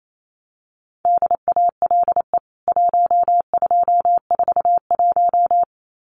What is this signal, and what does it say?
Morse code, heard as a keyed tone, about 28 wpm, 700 Hz. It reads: DALE 1241
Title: CWT Exchange 28wpm